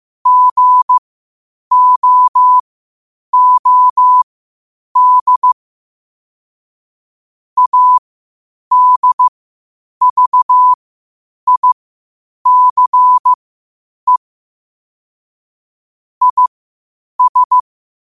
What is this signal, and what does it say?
Morse code, heard as a keyed tone, about 15 wpm, spaced as for 8 wpm, 1000 Hz. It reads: GOOD ADVICE IS
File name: qotdfiles / 8wpm